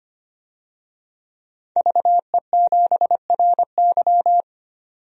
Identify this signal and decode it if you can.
Morse code, heard as a keyed tone, about 25 wpm, 700 Hz. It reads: VE7RY
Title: DXCC-100 Call Signs - 5 Characters - 25wpm